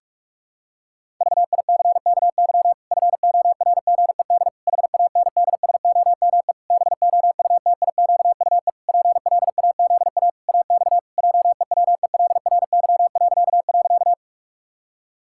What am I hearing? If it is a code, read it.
Morse code, heard as a keyed tone, about 46 wpm, 700 Hz. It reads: VICKY FORGED HANDSOME BOUTIQUE PLAZA AX JEWELRY..
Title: Pangram Challenge v2 - 45wpm